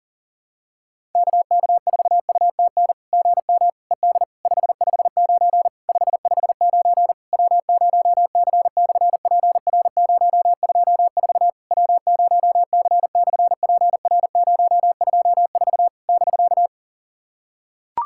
Morse code, heard as a keyed tone, about 38 wpm, 700 Hz. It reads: KK4UTN GM ED 559 559 W0C/PR024 W0C/PR024 <BK>
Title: SOTA QSO Practice 40wpm